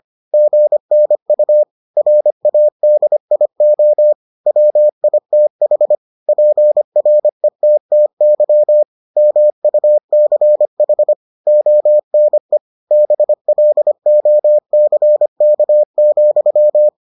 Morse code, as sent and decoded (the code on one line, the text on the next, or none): --. -. ..- / .-. .- -.. .. --- / .-- .. - .... / .--. .-. - - -.-- / -- ..- -.-. .... / --- -. / -... .-.. --- -.-. -.- --..--
GNU RADIO WITH PRETTY MUCH ONE BLOCK,